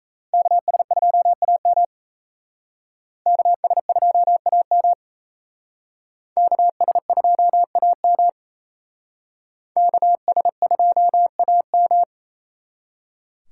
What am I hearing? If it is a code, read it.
Morse code, heard as a keyed tone, about 35 wpm, 700 Hz. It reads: KS2AM KS2AM KS2AM KS2AM
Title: Speed Racing - USA Call Signs 28wpm